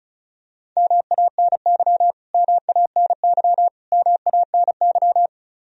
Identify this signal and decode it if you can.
Morse code, heard as a keyed tone, about 34 wpm, 700 Hz. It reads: MANY MANY MANY
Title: Top 100 Words Repeated 35wpm